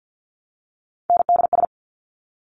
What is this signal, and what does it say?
Morse code, heard as a keyed tone, about 50 wpm, 700 Hz. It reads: NDS
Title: Top Three Letter Combinations in Words 50wpm